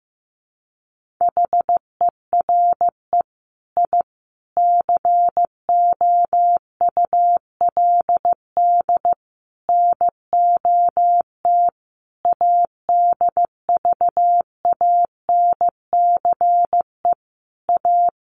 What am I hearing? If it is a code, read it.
Morse code, heard as a keyed tone, about 15 wpm, 700 Hz. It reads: HERE I COULD NOT ADVANCE A
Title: Sentences from Top 1400 Words 15wpm